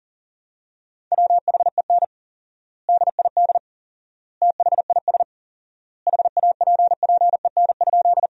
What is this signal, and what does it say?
Morse code, heard as a keyed tone, about 40 wpm, 700 Hz. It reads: WHEN DID THIS HAPPEN?
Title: 4x Word-Spacing - Sentences from Top 400 Words - 40wpm